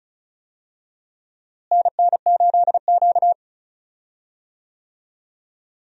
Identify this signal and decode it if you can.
Morse code, heard as a keyed tone, about 34 wpm, 700 Hz. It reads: NN8Q
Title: USA Call Signs - 4 Characters - Repeated 2x - 35wpm